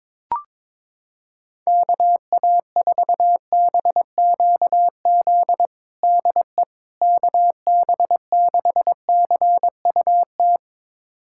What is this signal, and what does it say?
Morse code, heard as a keyed tone, about 21 wpm, 700 Hz. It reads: KA4BQZ DE KB6CUT
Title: USA Callsign DE Callsign - 6 Characters - 22wpm